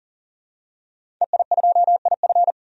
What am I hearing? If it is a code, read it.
Morse code, heard as a keyed tone, about 40 wpm, 700 Hz. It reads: EI2IF
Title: DXCC-100 Call Signs - 5 Characters - 40wpm